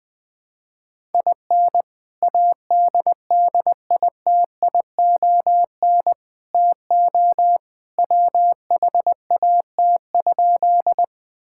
Morse code, heard as a keyed tone, about 21 wpm, 700 Hz. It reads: IN ADDITION TO WHAT?
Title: Sentences from Top 1000 Words - Review Part 6 - 20wpm